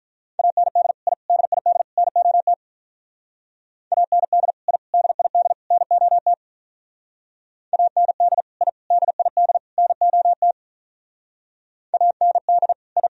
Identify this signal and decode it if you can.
Morse code, heard as a keyed tone, about 44 wpm, 700 Hz. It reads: AND I DID NOT AND I DID NOT AND I DID NOT AND I